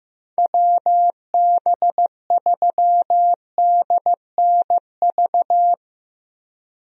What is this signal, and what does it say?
Morse code, heard as a keyed tone, about 15 wpm, 700 Hz. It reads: WB3DNV
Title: USA Call Signs - 6 Characters - Repeated 2x - 15wpm